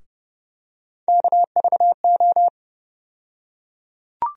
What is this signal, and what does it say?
Morse code, heard as a keyed tone, about 30 wpm, 700 Hz. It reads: KVO
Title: USA Call Sign Postfixes - 3 Letters 30wpm